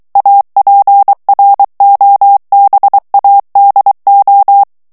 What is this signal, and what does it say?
Morse code, heard as a keyed tone, about 23 wpm, 800 Hz. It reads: APROBADO
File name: morse mp3 / ESPAGNOL TEXTE EN CLAIR 2 VITESSE 22 MOTS MINUTES